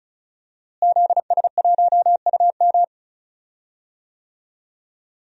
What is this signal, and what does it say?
Morse code, heard as a keyed tone, about 35 wpm, 700 Hz. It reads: ZS1UM
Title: DXCC-100 Call Signs - 5 Characters - Repeated 3x - 35wpm